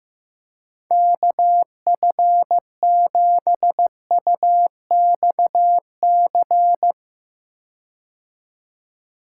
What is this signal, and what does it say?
Morse code, heard as a keyed tone, about 15 wpm, 700 Hz. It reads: KF7UXC
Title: K1USN SST QSO Practice - 15wpm